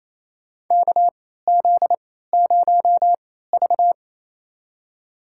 Morse code, heard as a keyed tone, about 28 wpm, spaced as for 16 wpm, 700 Hz. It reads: KZ0V
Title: USA Call Signs - 4 Characters - 3x Character Spacing - 28wpm